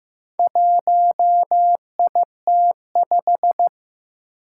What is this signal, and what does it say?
Morse code, heard as a keyed tone, about 15 wpm, 700 Hz. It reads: 1IT5
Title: Four Challenging Letter-Number Combinations 15wpm